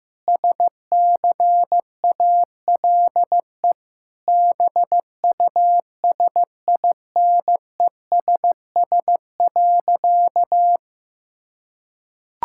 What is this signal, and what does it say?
Morse code, heard as a keyed tone, about 15 wpm, 700 Hz. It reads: SCALE BUSINESS.
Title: Sentences from Top 1000 Words - Review Part 5 - 15wpm